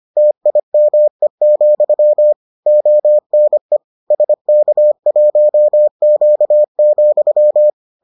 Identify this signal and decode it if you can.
Morse code, heard as a keyed tone, about 25 wpm, 600 Hz. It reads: TIME, ONE SK1Q,